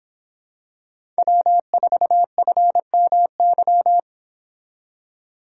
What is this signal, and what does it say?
Morse code, heard as a keyed tone, about 26 wpm, 700 Hz. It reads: W4FMY